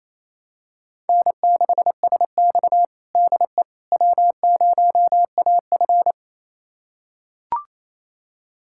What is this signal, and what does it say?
Morse code, heard as a keyed tone, about 28 wpm, 700 Hz. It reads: N6SX DE W0AF
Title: USA Callsign DE Callsign - 4 Characters - 28wpm